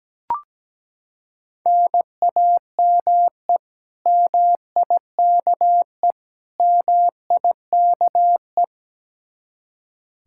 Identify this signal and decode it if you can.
Morse code, heard as a keyed tone, about 17 wpm, 700 Hz. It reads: NAME MIKE MIKE
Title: Practice QSO - Exchange 1 - Beginning - 17wpm